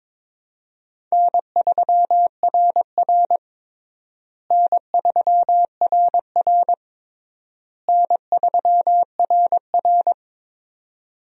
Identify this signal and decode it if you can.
Morse code, heard as a keyed tone, about 22 wpm, 700 Hz. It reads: N3RR N3RR N3RR
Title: USA Call Signs - 4 Characters - Repeated 3x - 22wpm